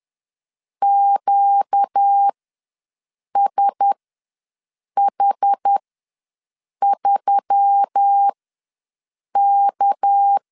Morse code, heard as a keyed tone, about 10 wpm, spaced as for 6 wpm, 800 Hz. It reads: QSH3K
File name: LEZIONI / 6 PKX0 / 6 PK ws 10wpm 3745